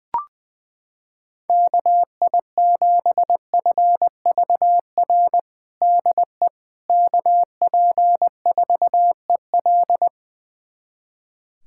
Morse code, heard as a keyed tone, about 20 wpm, 700 Hz. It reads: KI7FVR DE KP4EL
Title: USA Callsign DE Callsign 20wpm